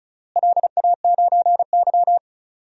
Morse code, heard as a keyed tone, about 36 wpm, 700 Hz. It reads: LA9Y
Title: DXCC-100 Call Signs - 4 Characters - Repeated 3x - 35wpm